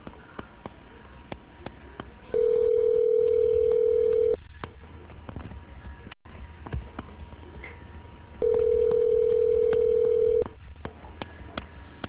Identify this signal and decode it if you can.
Ringback tone (North America), 440 + 480 Hz.